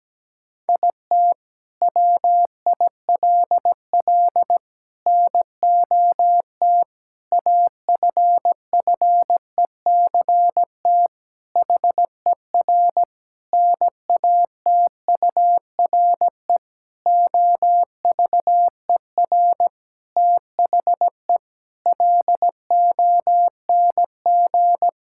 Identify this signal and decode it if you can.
Morse code, heard as a keyed tone, about 17 wpm, 700 Hz. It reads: IT WILL NOT AFFECT HER NATURE OVER THE LONG
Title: Sentences from Top 1000 Words - Review Part 6 - 17wpm